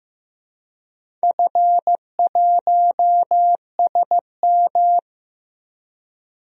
Morse code, heard as a keyed tone, about 15 wpm, 700 Hz. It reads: F1SM
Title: DXCC-100 Call Signs - 4 Characters - Repeated 2x - 15wpm